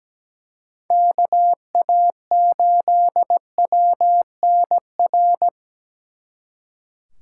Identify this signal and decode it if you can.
Morse code, heard as a keyed tone, about 20 wpm, 700 Hz. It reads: KA8WNR
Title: Speed Racing - USA 2x3 Call Signs 17wpm